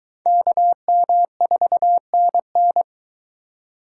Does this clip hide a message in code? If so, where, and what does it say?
Yes: Morse code from 0.3 to 2.8 s, heard as a keyed tone, about 22 wpm, 700 Hz. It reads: KM4NN